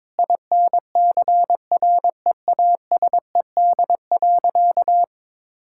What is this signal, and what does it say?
Morse code, heard as a keyed tone, about 22 wpm, 700 Hz. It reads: INCREASED.
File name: Sentences with 7-Word Phrases - 22wpm